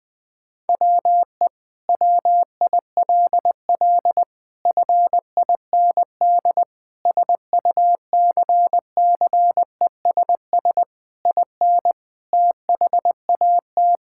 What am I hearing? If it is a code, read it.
Morse code, heard as a keyed tone, about 20 wpm, 700 Hz. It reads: WE WILL FIND SUCCESS IN THAT
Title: Sentences from Top 1000 Words - Review Part 5 - 20wpm